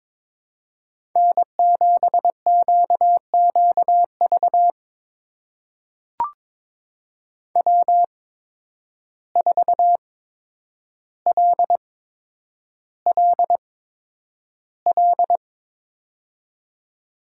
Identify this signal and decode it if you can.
Morse code, heard as a keyed tone, about 22 wpm, spaced as for 10 wpm, 700 Hz. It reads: N7QQV W4LLL
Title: USA Call Signs - 5 Characters - 8x Character Spacing - 22wpm